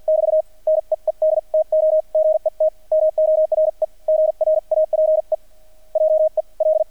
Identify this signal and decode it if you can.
Morse code, heard as a keyed tone, about 50 wpm, spaced as for 41 wpm, 630 Hz, with radio noise. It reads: <BT> NEED TO GET MORE GRAPE JEL
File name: archived / CRSnet